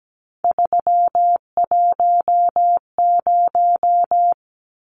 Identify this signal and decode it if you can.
Morse code, heard as a keyed tone, about 17 wpm, 700 Hz. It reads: 310
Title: Numbers - 3-Digit 17wpm